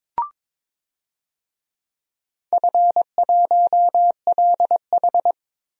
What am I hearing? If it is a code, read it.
Morse code, heard as a keyed tone, about 22 wpm, 700 Hz. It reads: F1LH
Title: DXCC-100 Call Signs - 4 Characters - 22wpm